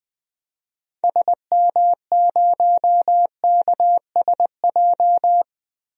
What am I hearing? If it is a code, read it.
Morse code, heard as a keyed tone, about 20 wpm, 700 Hz. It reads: SM0KSJ
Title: DXCC-100 Call Signs - 6 Characters - 20wpm